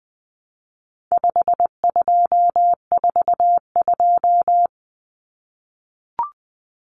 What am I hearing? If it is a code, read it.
Morse code, heard as a keyed tone, about 20 wpm, 700 Hz. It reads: 5242
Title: Numbers - 4-Digit 20wpm